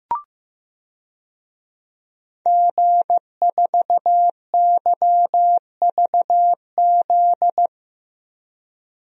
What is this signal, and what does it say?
Morse code, heard as a keyed tone, about 16 wpm, 700 Hz. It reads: G4YVZ
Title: DXCC-100 Call Signs - 5 Characters - Repeated 2x - 15wpm